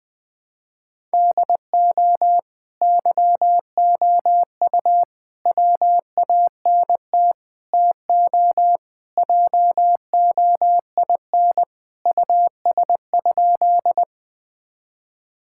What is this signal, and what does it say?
Morse code, heard as a keyed tone, about 20 wpm, 700 Hz. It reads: DO YOU WANT TO JOIN US?
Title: Sentences with 4-Word Phrases - Spoken Beginning - 20wpm